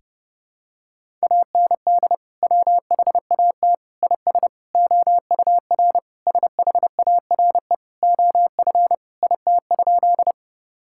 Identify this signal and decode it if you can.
Morse code, heard as a keyed tone, about 30 wpm, 700 Hz. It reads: AND WHAT IS OUR SHARE OF IT?
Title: Sentences from Top 1000 Words - Review Part 6 - 30wpm